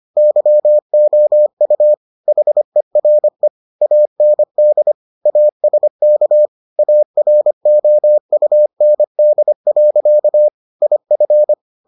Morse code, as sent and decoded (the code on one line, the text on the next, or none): -.-- --- ..- / .... .-. / .- -. -.. / .- ... -.- / .- .-. --- ..- -. -.. .-.-.- / .. ..-.
YOU HERE AND ASK AROUND. IF